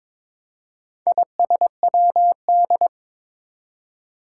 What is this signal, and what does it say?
Morse code, heard as a keyed tone, about 22 wpm, 700 Hz. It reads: ISWD
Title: Four Challenging Letter-Number Combinations 22wpm